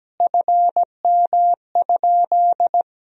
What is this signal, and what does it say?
Morse code, heard as a keyed tone, about 17 wpm, 700 Hz. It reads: FM?
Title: Sentences from Top 200 Words and Abbreviations 17wpm